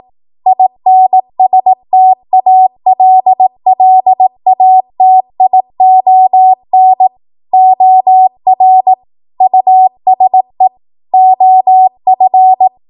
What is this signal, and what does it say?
Morse code, heard as a keyed tone, about 18 wpm, 750 Hz. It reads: INSTALLATION OR USE OF
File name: Morse / 18WPM